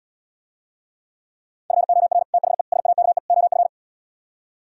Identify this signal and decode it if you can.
Morse code, heard as a keyed tone, about 77 wpm, 700 Hz. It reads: FOR THE SAKE OF